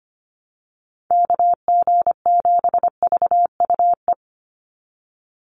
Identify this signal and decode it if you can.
Morse code, heard as a keyed tone, about 25 wpm, 700 Hz. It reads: KG7VUE
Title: USA Call Signs 25wpm